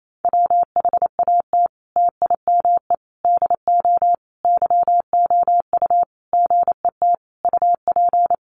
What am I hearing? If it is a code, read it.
Morse code, heard as a keyed tone, about 28 wpm, 700 Hz. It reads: WHAT TIME DO YOU GET UP
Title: Sentences from Top 1500 Words 28wpm